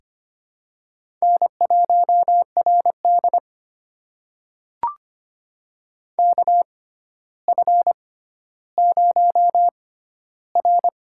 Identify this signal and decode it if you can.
Morse code, heard as a keyed tone, about 25 wpm, spaced as for 13 wpm, 700 Hz. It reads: N1RD KF0R
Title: USA Call Signs - 4 Characters - 6x Character Spacing - 25wpm